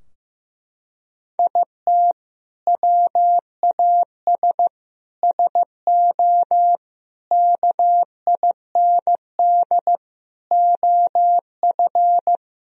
Morse code, heard as a keyed tone, about 15 wpm, 700 Hz. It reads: IT WAS SO KIND OF
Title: Sentences from Top 1000 Words - Review Part 3 - 15wpm